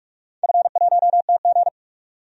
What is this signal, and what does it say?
Morse code, heard as a keyed tone, about 44 wpm, 700 Hz. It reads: F1TG
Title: DXCC-100 Call Signs - 4 Characters - 45wpm